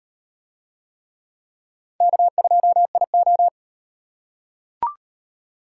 Morse code, heard as a keyed tone, about 38 wpm, 700 Hz. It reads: K2IO